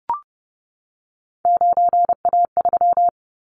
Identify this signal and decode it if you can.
Morse code, heard as a keyed tone, about 30 wpm, 700 Hz. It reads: 9A3
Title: Callsign Prefixes for Top 25 DXCC - 30wpm